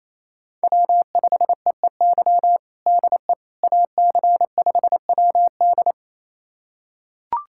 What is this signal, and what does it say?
Morse code, heard as a keyed tone, about 28 wpm, 700 Hz. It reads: W5EEY DE AC5WD
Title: USA Callsign DE Callsign - 5 Characters - 28wpm